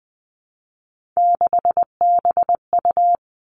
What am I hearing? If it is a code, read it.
Morse code, heard as a keyed tone, about 20 wpm, 700 Hz. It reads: 6BU
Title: Three Challenging Letter-Number Combinations 20wpm